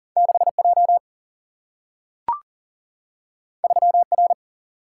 Morse code, heard as a keyed tone, about 39 wpm, 700 Hz. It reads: 6J 3R
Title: Single-Letter Single-Number 40wpm